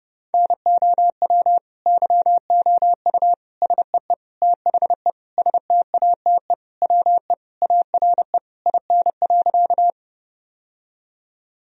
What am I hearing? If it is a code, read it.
Morse code, heard as a keyed tone, about 32 wpm, 700 Hz. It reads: NOW YOU SEE THE STATE WE ARE IN.
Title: Sentences from Top 1000 Words - Review Part 2 - 30wpm